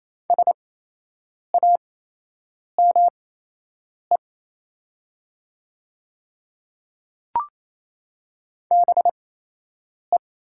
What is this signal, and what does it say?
Morse code, heard as a keyed tone, about 28 wpm, spaced as for 8 wpm, 700 Hz. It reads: SAME BE
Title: 8x Character Spacing - Top 200 Words - 28wpm